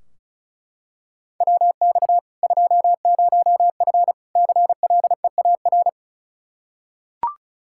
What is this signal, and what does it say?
Morse code, heard as a keyed tone, about 36 wpm, 700 Hz. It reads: WX 20F CLEAR
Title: K1USN SST QSO Practice 35wpm